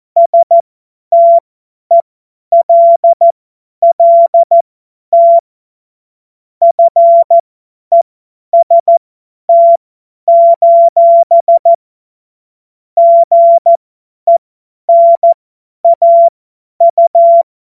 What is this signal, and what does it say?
Morse code, heard as a keyed tone, about 14 wpm, spaced as for 10 wpm, 680 Hz. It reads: STELLT FEST: GENAU
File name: Morsekurs / Witze / Witz 21. Juni 2022 50 F70